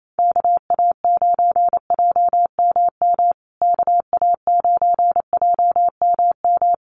Morse code, heard as a keyed tone, about 28 wpm, 700 Hz. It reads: KA9JMM KA9JMM
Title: USA Call Signs Repeated 2x 28wpm